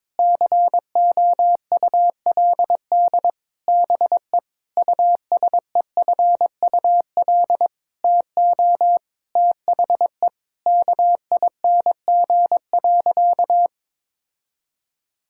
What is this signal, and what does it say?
Morse code, heard as a keyed tone, about 22 wpm, 700 Hz. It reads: COULD BE USEFUL TO THE KING.
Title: Sentences from Top 1000 Words - Review Part 6 - 22wpm